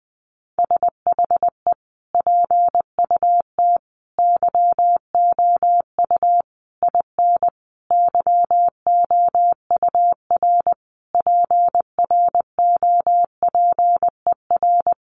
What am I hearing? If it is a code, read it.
Morse code, heard as a keyed tone, about 20 wpm, 700 Hz. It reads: SHE PUT YOU IN YOUR PROPER
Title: Sentences from Top 800 Words 20wpm